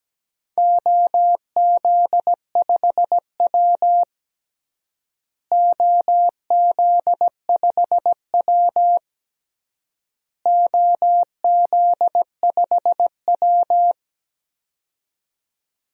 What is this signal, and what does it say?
Morse code, heard as a keyed tone, about 17 wpm, 700 Hz. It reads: OZ5W OZ5W OZ5W
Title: DXCC-100 Call Signs - 4 Characters - Repeated 3x - 17wpm